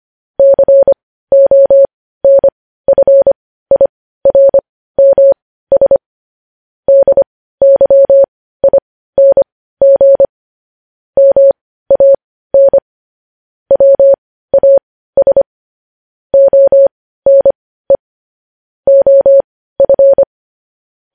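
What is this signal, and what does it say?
Morse code, heard as a keyed tone, about 25 wpm, spaced as for 15 wpm, 550 Hz. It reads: CONFIRMS DYING MAN WAS ONE OF
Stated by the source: Morse news headlines sent at 15 wpm.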